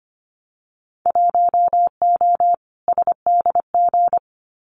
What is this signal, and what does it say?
Morse code, heard as a keyed tone, about 25 wpm, 700 Hz. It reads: 1O SDG
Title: Winter Field Day Exchanges Encore 25wpm